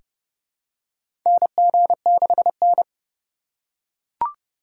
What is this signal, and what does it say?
Morse code, heard as a keyed tone, about 30 wpm, 700 Hz. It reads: NG6N
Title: USA Call Signs - 4 Characters - 3x Character Spacing - 30wpm